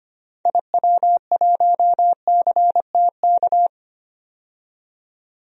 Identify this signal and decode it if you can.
Morse code, heard as a keyed tone, about 25 wpm, 700 Hz. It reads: IW1CTK
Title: DXCC-100 Call Signs - 6 Characters - 25wpm